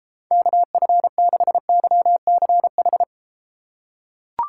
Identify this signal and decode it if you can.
Morse code, heard as a keyed tone, about 32 wpm, 700 Hz. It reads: KF6YCH